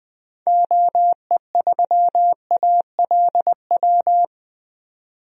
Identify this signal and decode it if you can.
Morse code, heard as a keyed tone, about 21 wpm, 700 Hz. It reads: OE3ALW
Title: DXCC-100 Call Signs - 6 Characters - 20wpm